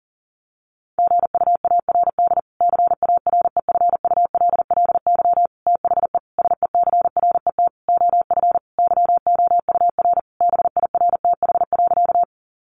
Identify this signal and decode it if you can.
Morse code, heard as a keyed tone, about 40 wpm, 700 Hz. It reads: GUARD CAREFULLY THE SECRET OF YOUR BIRTH.
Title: Sentences from Top 1300 Words 40wpm